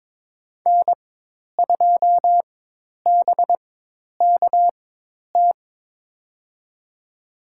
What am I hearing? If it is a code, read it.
Morse code, heard as a keyed tone, about 21 wpm, spaced as for 10 wpm, 700 Hz. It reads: N2BKT
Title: USA Call Signs - 5 Characters - 4x Character Spacing - 22wpm